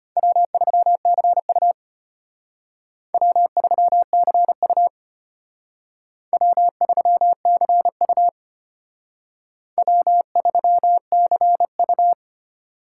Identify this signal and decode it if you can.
Morse code, heard as a keyed tone, about 32 wpm, 700 Hz. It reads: W3CU W3CU W3CU W3CU